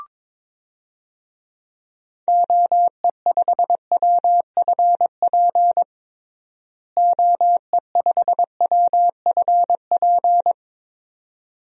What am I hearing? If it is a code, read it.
Morse code, heard as a keyed tone, about 22 wpm, 700 Hz. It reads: OE5WFP OE5WFP
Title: DXCC-100 Call Signs - 6 Characters - Repeated 2x - 22wpm